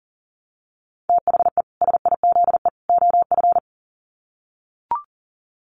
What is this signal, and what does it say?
Morse code, heard as a keyed tone, about 39 wpm, 700 Hz. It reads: THE SIZE OF
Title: Sets of 3 Words 40wpm